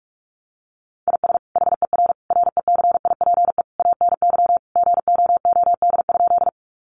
Morse code, heard as a keyed tone, about 44 wpm, 700 Hz. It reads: IS HER RECIPE ANY GOOD?
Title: Sentences from Top 1200 Words 45wpm